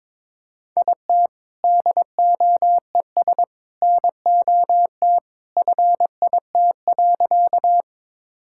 Morse code, heard as a keyed tone, about 22 wpm, 700 Hz. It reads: IT DOES NOT FIT.
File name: practice / 22wpm / Sentences with 2-Word Phrases - 22wpm